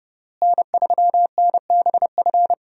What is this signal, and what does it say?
Morse code, heard as a keyed tone, about 30 wpm, 700 Hz. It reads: N3NBF
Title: USA Call Signs - 5 Characters - Repeated 2x - 30wpm